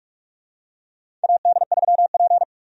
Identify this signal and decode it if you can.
Morse code, heard as a keyed tone, about 47 wpm, 700 Hz. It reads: AD3P
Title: USA Call Signs - 4 Characters - 45wpm